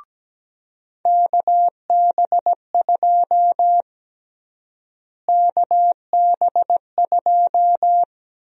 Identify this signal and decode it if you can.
Morse code, heard as a keyed tone, about 17 wpm, 700 Hz. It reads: KB2 KB2
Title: USA Call Sign Prefixes - Repeated 2x - 17wpm